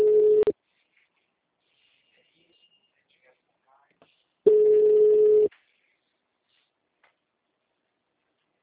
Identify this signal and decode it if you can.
Ringback tone (Europe, 425 Hz), with speech.
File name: out- -2041- - - .57770